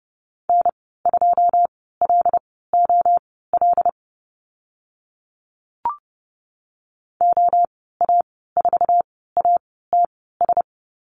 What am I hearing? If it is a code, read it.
Morse code, heard as a keyed tone, about 30 wpm, spaced as for 17 wpm, 700 Hz. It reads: N2LOL OA4ATS
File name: Lesson 14g - Number 2 - Call Signs 30wpm